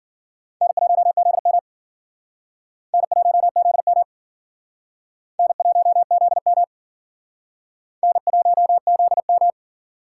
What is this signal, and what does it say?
Morse code, heard as a keyed tone, about 50 wpm, 700 Hz. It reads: N1ZM N1ZM N1ZM N1ZM